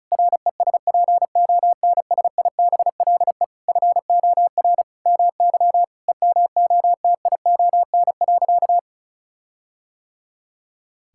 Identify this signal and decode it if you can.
Morse code, heard as a keyed tone, about 36 wpm, 700 Hz. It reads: RESPONSIBLE FOR MY EMOTION.
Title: Sentences from Top 1600 Words 35wpm